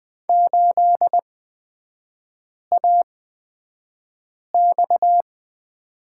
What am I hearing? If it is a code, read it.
Morse code, heard as a keyed tone, about 20 wpm, spaced as for 5 wpm, 690 Hz. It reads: 8AX
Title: 229 Lesson 37e - Letter x - Three Characters 20wpm